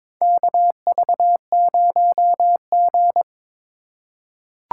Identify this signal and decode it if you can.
Morse code, heard as a keyed tone, about 22 wpm, 700 Hz. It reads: KV0G
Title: USA Call Signs - 4 Characters - 4x Character Spacing - 22wpm